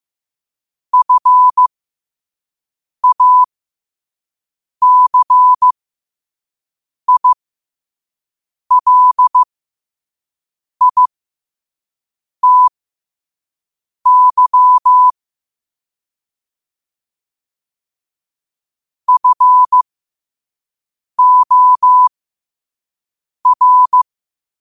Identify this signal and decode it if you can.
Morse code, heard as a keyed tone, about 15 wpm, spaced as for 5 wpm, 1000 Hz. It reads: FACILITY FOR
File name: qotdfiles / 5wpm